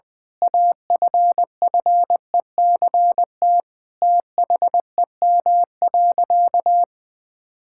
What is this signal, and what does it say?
Morse code, heard as a keyed tone, about 20 wpm, 700 Hz. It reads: AFFECT THEM.
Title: Sentences from Top 1000 Words - Review Part 6 - 20wpm